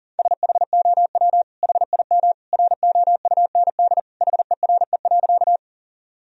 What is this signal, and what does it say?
Morse code, heard as a keyed tone, about 40 wpm, 700 Hz. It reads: SHOW HIM ROUND HERE.
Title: Sentences from Top 1000 Words - Review Part 1 - 40wpm